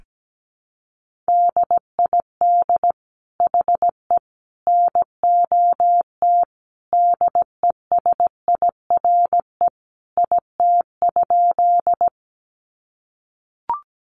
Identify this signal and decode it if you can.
Morse code, heard as a keyed tone, about 17 wpm, 700 Hz. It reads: DID HE NOT DESIRE IT?
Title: Sentences from Top 1100 Words 17wpm